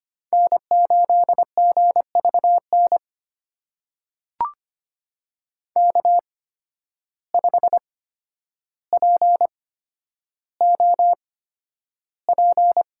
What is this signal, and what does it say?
Morse code, heard as a keyed tone, about 25 wpm, spaced as for 11 wpm, 700 Hz. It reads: N8GVN K5POP
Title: USA Call Signs - 5 Characters - 8x Character Spacing - 25wpm